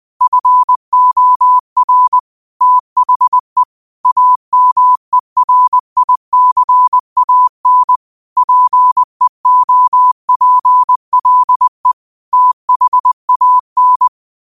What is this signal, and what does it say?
Morse code, heard as a keyed tone, about 20 wpm, 1000 Hz. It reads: FOR THE AMERICAN PEOPLE THAN